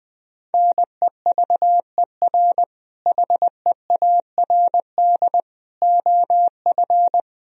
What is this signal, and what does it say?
Morse code, heard as a keyed tone, about 20 wpm, 700 Hz. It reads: NEVER HEARD OF
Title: Sentences from Top 400 Words and Abbreviations 20wpm